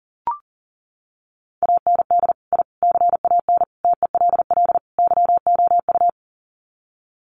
Morse code, heard as a keyed tone, about 40 wpm, 700 Hz. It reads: AND I CAN TELL YOU
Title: Sets of 5 Words Encore2 40wpm